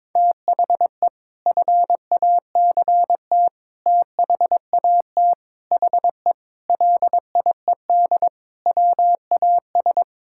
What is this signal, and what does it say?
Morse code, heard as a keyed tone, about 22 wpm, 700 Hz. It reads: THE FACT THAT HE LIED WAS
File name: practice / 22wpm / Sentences with 4-Word Phrases - 22wpm